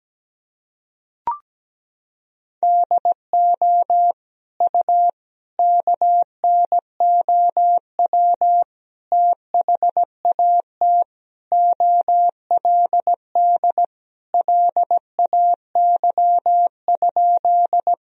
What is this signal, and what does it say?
Morse code, heard as a keyed tone, about 17 wpm, 700 Hz. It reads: DO U KNOW THAT OLD LAY?
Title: Sentences from Top 400 Words and Abbreviations 17wpm